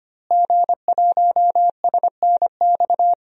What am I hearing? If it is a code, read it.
Morse code, heard as a keyed tone, about 25 wpm, 700 Hz. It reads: G1SNX
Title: DXCC-100 Call Signs - 5 Characters - Repeated 3x - 25wpm